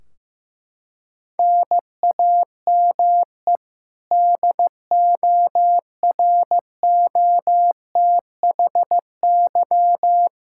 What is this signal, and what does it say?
Morse code, heard as a keyed tone, about 15 wpm, 700 Hz. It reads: NAME DOROTHY
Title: Practice QSO - Exchange 1 - Beginning - 15wpm